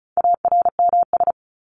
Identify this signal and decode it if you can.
Morse code, heard as a keyed tone, about 34 wpm, 700 Hz. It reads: ARMS
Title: ES:S Suffix - Top 100 Words with Base Words 35wpm